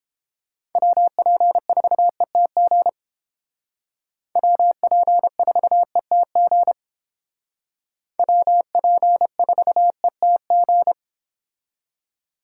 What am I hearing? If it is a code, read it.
Morse code, heard as a keyed tone, about 31 wpm, 700 Hz. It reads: WP4ETG WP4ETG WP4ETG